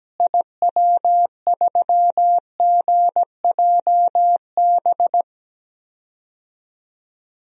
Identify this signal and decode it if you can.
Morse code, heard as a keyed tone, about 17 wpm, 700 Hz. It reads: IW3GJB
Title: DXCC-100 Call Signs - 6 Characters - Repeated 2x - 17wpm